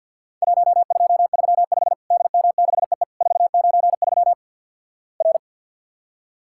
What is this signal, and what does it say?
Morse code, heard as a keyed tone, about 50 wpm, 700 Hz. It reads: 1235 DM6EE 493 R
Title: Worked All Europe DX Contest - QTC Exchange - 50wpm